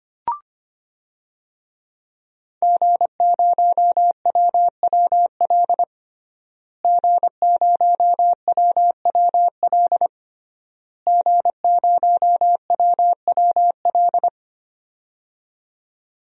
Morse code, heard as a keyed tone, about 25 wpm, 700 Hz. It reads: G0WWL G0WWL G0WWL
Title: DXCC-100 Call Signs - 5 Characters - Repeated 3x - 25wpm